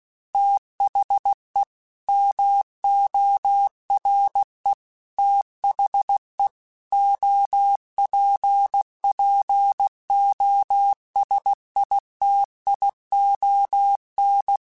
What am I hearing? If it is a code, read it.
Morse code, heard as a keyed tone, about 16 wpm, 800 Hz. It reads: THE MORE THE OPPOSITION
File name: Technical Subjects / Technical Subjects-mp3s / XL @15